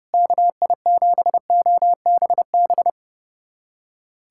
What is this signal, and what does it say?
Morse code, heard as a keyed tone, about 30 wpm, 700 Hz. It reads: KI7OBB
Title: USA Callsign DE Callsign - 6 Characters - 30wpm